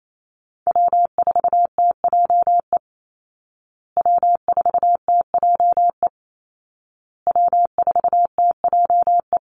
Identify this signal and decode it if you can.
Morse code, heard as a keyed tone, about 28 wpm, 700 Hz. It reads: W4TJE W4TJE W4TJE
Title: Call Signs Repeated 2x - 28wpm